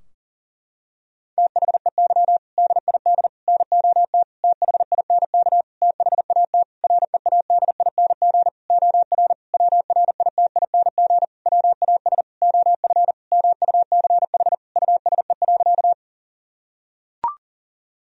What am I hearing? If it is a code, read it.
Morse code, heard as a keyed tone, about 40 wpm, 700 Hz. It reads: THEY DID NOT THINK THAT READING OR WRITING WAS OF MUCH USE.
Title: Sentences from Top 1000 Words - Review Part 3 - 40wpm